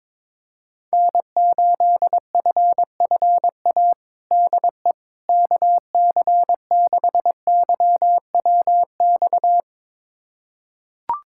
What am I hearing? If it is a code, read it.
Morse code, heard as a keyed tone, about 22 wpm, 700 Hz. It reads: N8FFA DE KC6YWX
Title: USA Callsign DE Callsign 22wpm